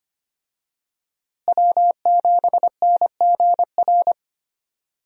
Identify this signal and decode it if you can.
Morse code, heard as a keyed tone, about 25 wpm, 700 Hz. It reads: W7NGR
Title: USA Call Signs - 5 Characters - Repeated 2x - 25wpm